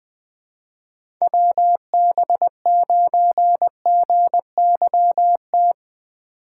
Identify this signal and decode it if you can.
Morse code, heard as a keyed tone, about 20 wpm, 700 Hz. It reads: WB9GYT
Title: USA Call Signs - 6x Character Spacing - 20wpm